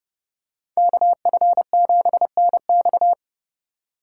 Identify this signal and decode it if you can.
Morse code, heard as a keyed tone, about 29 wpm, 700 Hz. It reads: KF7NX